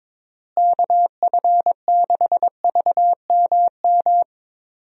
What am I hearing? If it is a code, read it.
Morse code, heard as a keyed tone, about 22 wpm, 700 Hz. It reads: KF6VMM
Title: USA Call Signs - 6 Characters - Repeated 3x - 22wpm